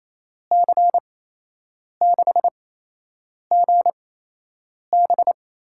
Morse code, heard as a keyed tone, about 28 wpm, spaced as for 8 wpm, 700 Hz. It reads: C6GB
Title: USA Call Signs - 5 Characters - 8x Character Spacing - 28wpm